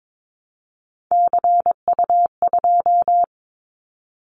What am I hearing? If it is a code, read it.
Morse code, heard as a keyed tone, about 22 wpm, 700 Hz. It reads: CU2
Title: Callsign Prefixes for Top 50 DXCC - 22wpm